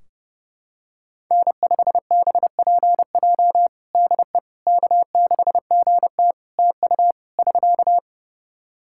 Morse code, heard as a keyed tone, about 32 wpm, 700 Hz. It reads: N5BPJ DE K6GT TU <SK>
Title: K1USN SST QSO Practice 30wpm